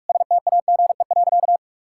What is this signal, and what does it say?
Morse code, heard as a keyed tone, about 46 wpm, 700 Hz. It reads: STAGE.
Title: Sentences from Top 1600 Words 45wpm